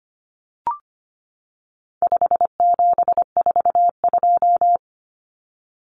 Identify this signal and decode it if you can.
Morse code, heard as a keyed tone, about 25 wpm, 700 Hz. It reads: 5742
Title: Common Cut Numbers - 4-Digit 25wpm